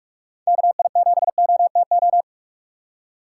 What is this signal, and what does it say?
Morse code, heard as a keyed tone, about 45 wpm, 700 Hz. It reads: KI7OTO
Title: USA Call Signs - 6 Characters - 3x Character Spacing - 45wpm